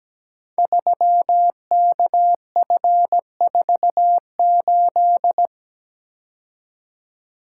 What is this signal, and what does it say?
Morse code, heard as a keyed tone, about 17 wpm, 700 Hz. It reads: 3KF48
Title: Code Groups - 5 Letter:Numbers - 17wpm